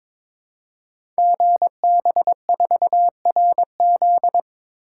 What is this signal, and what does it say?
Morse code, heard as a keyed tone, about 22 wpm, 700 Hz. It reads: GB4RZ
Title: Code Groups - 5 Letter:Numbers - 22wpm